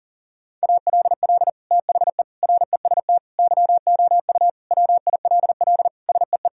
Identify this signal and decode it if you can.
Morse code, heard as a keyed tone, about 40 wpm, 700 Hz. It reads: ALL THE REST YOU WILL SEE
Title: Sentences from Top 1000 Words - Review Part 4 - 40wpm